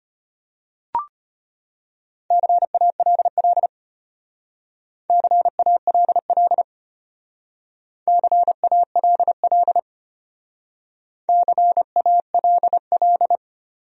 Morse code, heard as a keyed tone, about 32 wpm, 700 Hz. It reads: CALL CALL CALL CALL